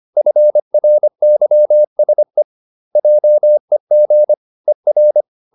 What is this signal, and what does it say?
Morse code, heard as a keyed tone, about 25 wpm, 600 Hz. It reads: FRYSE JEG ER
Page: Ferie 25 wpm